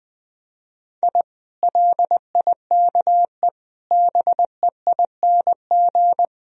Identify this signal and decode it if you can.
Morse code, heard as a keyed tone, about 21 wpm, 700 Hz. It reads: I LIKE BEING
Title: Sentences from Top 1000 Words - Review Part 3 - 20wpm